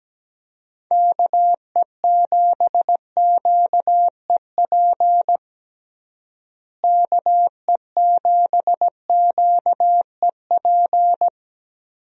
Morse code, heard as a keyed tone, about 17 wpm, 700 Hz. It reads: KE7QEP KE7QEP
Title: USA Call Signs - 6 Characters - Repeated 3x - 17wpm